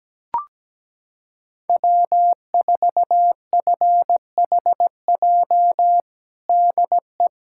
Morse code, heard as a keyed tone, about 17 wpm, 700 Hz. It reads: W4FHJ DE
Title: USA Callsign DE Callsign - 5 Characters - 17wpm